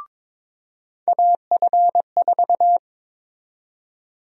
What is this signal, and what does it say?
Morse code, heard as a keyed tone, about 22 wpm, 700 Hz. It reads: AF4
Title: USA Call Sign Prefixes - 22wpm